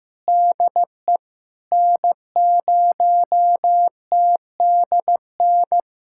Morse code, heard as a keyed tone, about 15 wpm, 700 Hz. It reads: DE N0TDN
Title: SOTA QSO Practice 15wpm